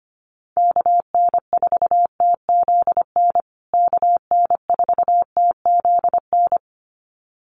Morse code, heard as a keyed tone, about 25 wpm, 700 Hz. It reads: KN4TZN KN4TZN
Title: USA Call Signs Repeated 2x 25wpm